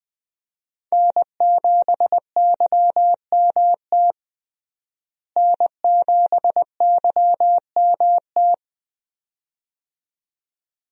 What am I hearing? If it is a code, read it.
Morse code, heard as a keyed tone, about 20 wpm, 700 Hz. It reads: N7YMT N7YMT
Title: USA Call Signs - 5 Characters - Repeated 2x - 20wpm